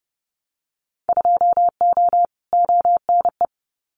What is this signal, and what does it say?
Morse code, heard as a keyed tone, about 30 wpm, 700 Hz. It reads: 2O ONE
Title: Winter Field Day Exchanges Encore 30wpm